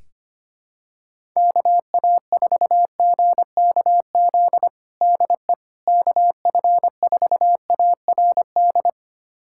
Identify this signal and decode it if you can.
Morse code, heard as a keyed tone, about 25 wpm, 700 Hz. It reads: KA4GKZ DE KF4ARD
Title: USA Callsign DE Callsign - 6 Characters - 25wpm